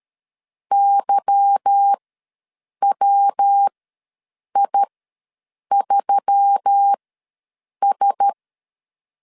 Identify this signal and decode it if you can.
Morse code, heard as a keyed tone, about 13 wpm, spaced as for 7 wpm, 800 Hz. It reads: YWI3S
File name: LEZIONI / 10 SIGNS / 10D COMA ws 12wpm 3618